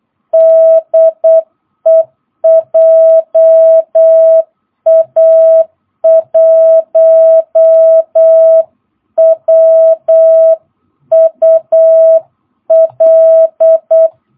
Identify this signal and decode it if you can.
Morse code, heard as a keyed tone, about 8 wpm, 650 Hz. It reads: DE JA1WUL